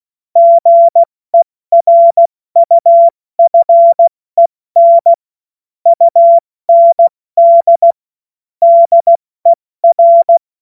Morse code, heard as a keyed tone, about 16 wpm, 680 Hz. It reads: GERUFEN UND DER
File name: Morsekurs / Witze / Witz 19. Oktober 2021, 70 F80